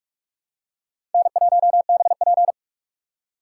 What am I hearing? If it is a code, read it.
Morse code, heard as a keyed tone, about 44 wpm, 700 Hz. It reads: N1BP